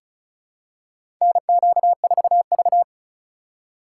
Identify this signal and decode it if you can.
Morse code, heard as a keyed tone, about 36 wpm, 700 Hz. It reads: NQ4V
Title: USA Call Signs - 4 Characters - 4x Character Spacing - 35wpm